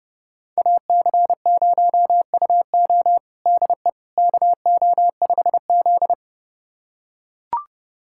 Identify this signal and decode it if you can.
Morse code, heard as a keyed tone, about 32 wpm, 700 Hz. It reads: AC0UO DE KO5Z
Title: USA Callsign DE Callsign 30wpm